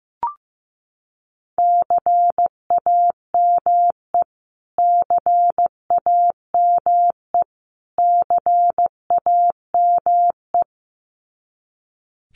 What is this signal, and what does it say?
Morse code, heard as a keyed tone, about 15 wpm, 700 Hz. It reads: CAME CAME CAME
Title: Top 200 Words Repeated 15wpm